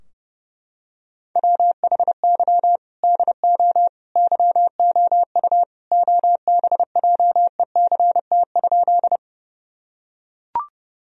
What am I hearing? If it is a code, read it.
Morse code, heard as a keyed tone, about 30 wpm, 700 Hz. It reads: WHY DO YOU OBJECT?
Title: Sentences from Top 1000 Words - Review Part 3 - 30wpm